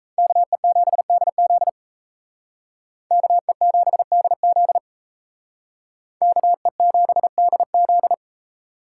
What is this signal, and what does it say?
Morse code, heard as a keyed tone, about 35 wpm, 700 Hz. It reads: KE7DZ KE7DZ KE7DZ
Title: Speed Racing - USA 2x2 Call Signs 28wpm